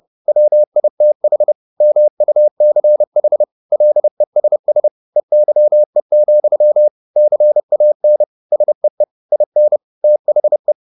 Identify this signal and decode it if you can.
Morse code, heard as a keyed tone, about 30 wpm, 600 Hz. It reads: WITH MUCH LESS EYE, CAN SEE IN THE
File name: the old man and the sea / the old man and the sea 1 30wpm